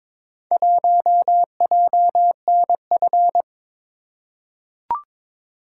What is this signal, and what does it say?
Morse code, heard as a keyed tone, about 22 wpm, 700 Hz. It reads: 1JNF
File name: practice / 22wpm / Four Letter-Number Combinations - 22wpm